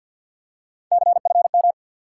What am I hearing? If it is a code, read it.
Morse code, heard as a keyed tone, about 50 wpm, 700 Hz. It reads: CFM
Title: Top 180 QSO Elements 50wpm